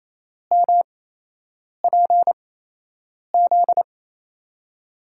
Morse code, heard as a keyed tone, about 28 wpm, spaced as for 8 wpm, 700 Hz. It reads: MPZ
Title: USA Call Signs - 5 Characters - 8x Character Spacing - 28wpm